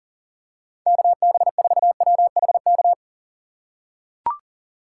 Morse code, heard as a keyed tone, about 40 wpm, 700 Hz. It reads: KB4WHK
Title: USA Call Signs - 6 Characters - 3x Character Spacing - 40wpm